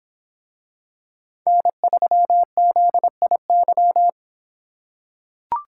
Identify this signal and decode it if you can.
Morse code, heard as a keyed tone, about 24 wpm, 700 Hz. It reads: N3ZIY